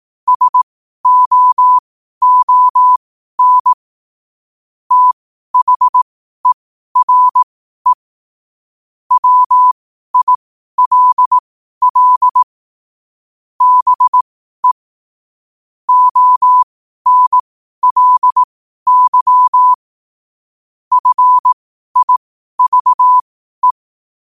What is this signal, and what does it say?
Morse code, heard as a keyed tone, about 18 wpm, spaced as for 12 wpm, 1000 Hz. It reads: SOON THERE WILL BE ONLY FIVE
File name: qotdfiles / 12wpm